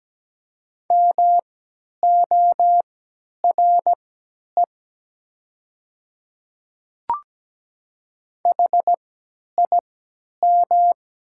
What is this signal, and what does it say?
Morse code, heard as a keyed tone, about 17 wpm, spaced as for 10 wpm, 700 Hz. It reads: MORE HIM
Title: 3x Character Spacing - Top 100 Words - 17wpm